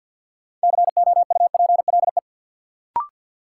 Morse code, heard as a keyed tone, about 53 wpm, 700 Hz. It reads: COUPLE
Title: Sets of 2 Words - New Word Drill - 50wpm